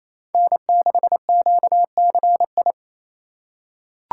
Morse code, heard as a keyed tone, about 27 wpm, 700 Hz. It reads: N6QCI
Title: USA Call Signs - 5 Characters - 2x Character Spacing - 28wpm